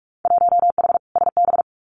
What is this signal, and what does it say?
Morse code, heard as a keyed tone, about 44 wpm, 700 Hz. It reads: 1H SB
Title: Winter Field Day Exchanges 45wpm